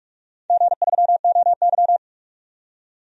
Morse code, heard as a keyed tone, about 42 wpm, 700 Hz. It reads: G3OY
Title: DXCC-100 Call Signs - 4 Characters - 45wpm